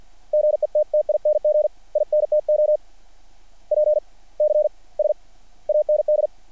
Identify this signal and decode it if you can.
Morse code, heard as a keyed tone, about 50 wpm, spaced as for 39 wpm, 600 Hz. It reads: GETTING INTO PCS AND
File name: archived / CRSnet